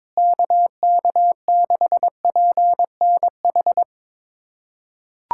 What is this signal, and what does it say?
Morse code, heard as a keyed tone, about 22 wpm, 700 Hz. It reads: KK6PNH
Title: USA Callsign DE Callsign - 6 Characters - 22wpm